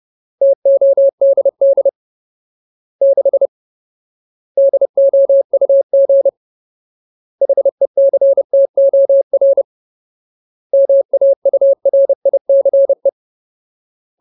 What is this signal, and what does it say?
Morse code, heard as a keyed tone, about 30 wpm, 550 Hz. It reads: TODD 6 DOUG HECTOR MAURICE
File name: Ultimate Word List / 30WPM Ultimate